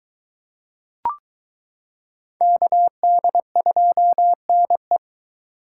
Morse code, heard as a keyed tone, about 22 wpm, 700 Hz. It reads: KD2NE